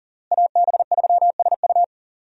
Morse code, heard as a keyed tone, about 40 wpm, 700 Hz. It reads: AB3SU
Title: USA Call Signs - 5 Characters - Repeated 3x - 40wpm